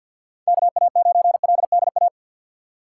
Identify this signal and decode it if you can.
Morse code, heard as a keyed tone, about 52 wpm, 700 Hz. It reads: KA9LDA
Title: USA Call Signs - 6 Characters - 3x Character Spacing - 50wpm